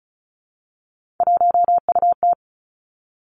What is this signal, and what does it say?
Morse code, heard as a keyed tone, about 36 wpm, 700 Hz. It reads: 1UT
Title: Three Challenging Letter-Number Combinations 35wpm